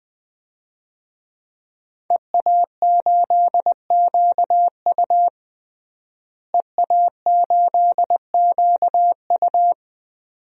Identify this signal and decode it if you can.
Morse code, heard as a keyed tone, about 20 wpm, 700 Hz. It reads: EA8QU EA8QU
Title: DXCC-100 Call Signs - 5 Characters - Repeated 2x - 20wpm